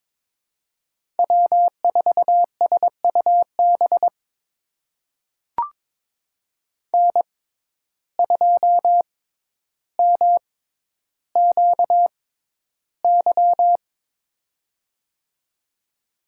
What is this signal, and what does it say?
Morse code, heard as a keyed tone, about 22 wpm, spaced as for 11 wpm, 700 Hz. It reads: W4SUB N2MQY
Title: USA Call Signs - 5 Characters - 6x Character Spacing - 22wpm